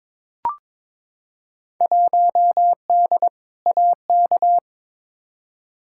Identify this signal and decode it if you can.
Morse code, heard as a keyed tone, about 22 wpm, 700 Hz. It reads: 1D AK
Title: Field Day Exchanges - 22wpm